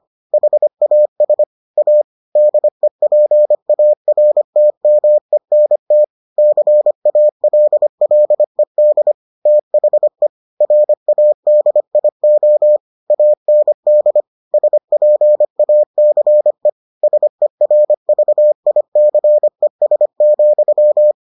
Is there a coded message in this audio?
Yes: Morse code from 0.3 to 21.1 s, heard as a keyed tone, about 25 wpm, 600 Hz. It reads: HAS A DEPARTMENT CALLED THE RADIO AND SPACE SERVICES,